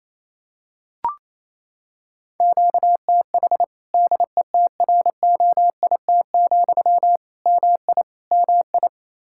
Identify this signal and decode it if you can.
Morse code, heard as a keyed tone, about 28 wpm, 700 Hz. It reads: QTH DETROIT, MI MI
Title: Practice QSO - Exchange 1 - Beginning - 28wpm